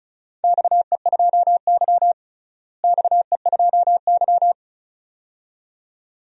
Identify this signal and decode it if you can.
Morse code, heard as a keyed tone, about 36 wpm, 700 Hz. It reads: XE2Y XE2Y
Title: DXCC-100 Call Signs - 4 Characters - Repeated 2x - 35wpm